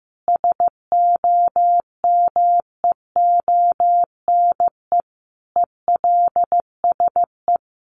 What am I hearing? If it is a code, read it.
Morse code, heard as a keyed tone, about 15 wpm, 700 Hz. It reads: SOMEONE ELSE
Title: Sentences from Top 1200 Words 15wpm